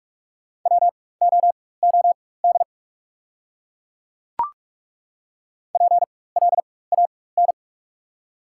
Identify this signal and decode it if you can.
Morse code, heard as a keyed tone, about 43 wpm, spaced as for 21 wpm, 690 Hz. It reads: WOOD PLAN
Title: 4x Character Spacing - Top 500 Words - 45wpm